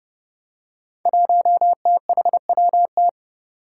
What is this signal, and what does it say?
Morse code, heard as a keyed tone, about 30 wpm, 700 Hz. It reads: 1THWT
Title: Five Challenging Letter-Number Combinations 30wpm